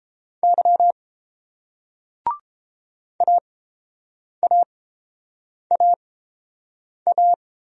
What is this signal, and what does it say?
Morse code, heard as a keyed tone, about 30 wpm, spaced as for 7 wpm, 700 Hz. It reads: Y AAAA